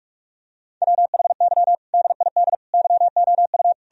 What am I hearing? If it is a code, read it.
Morse code, heard as a keyed tone, about 46 wpm, 700 Hz. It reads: WHY DID YOU
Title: Sentences from Top 1000 Words - Review Part 4 - 45wpm